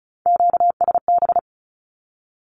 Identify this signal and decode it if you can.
Morse code, heard as a keyed tone, about 34 wpm, 700 Hz. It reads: QSB
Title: Q-Codes 35wpm